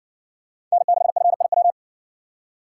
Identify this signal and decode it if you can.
Morse code, heard as a keyed tone, about 53 wpm, 700 Hz. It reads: N6FIW